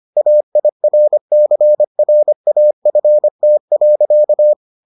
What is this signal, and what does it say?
Morse code, heard as a keyed tone, about 25 wpm, 600 Hz. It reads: AIRCRAFT.